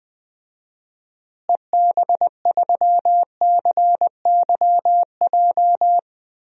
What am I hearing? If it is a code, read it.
Morse code, heard as a keyed tone, about 20 wpm, 700 Hz. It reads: EB3CYJ
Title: DXCC-100 Call Signs - 6 Characters - Repeated 2x - 20wpm